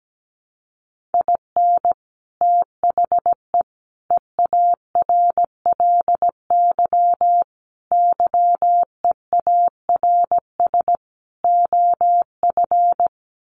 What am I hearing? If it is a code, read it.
Morse code, heard as a keyed tone, about 17 wpm, 690 Hz. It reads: IN THE EARLY YEARS OF
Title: Sets of 5 Words Encore 17wpm